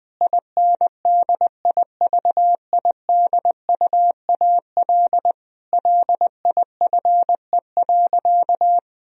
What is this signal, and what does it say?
Morse code, heard as a keyed tone, about 20 wpm, 700 Hz. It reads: INDIVIDUAL LIFE.
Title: Sentences from Top 1000 Words - Review Part 4 - 20wpm